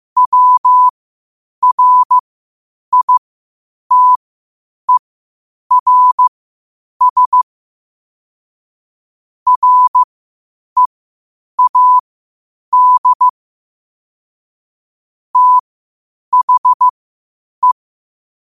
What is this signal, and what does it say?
Morse code, heard as a keyed tone, about 15 wpm, spaced as for 8 wpm, 1000 Hz. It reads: WRITERS READ THE
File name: qotdfiles / 8wpm